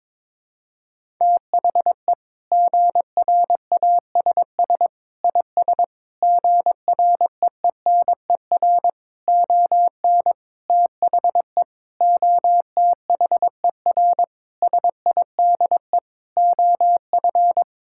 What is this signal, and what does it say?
Morse code, heard as a keyed tone, about 22 wpm, 700 Hz. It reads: THE GRASS IS GREENER ON THE OTHER SIDE OF
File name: practice / 22wpm / Sentences with 4-Word Phrases - 22wpm